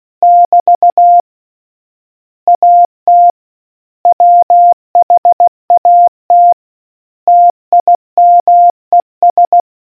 Morse code, heard as a keyed tone, about 16 wpm, 700 Hz. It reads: <BT> AT WHAT TIMES